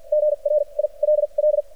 Morse code, heard as a keyed tone, about 47 wpm, spaced as for 35 wpm, 600 Hz. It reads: GRILL
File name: archived / CRSnet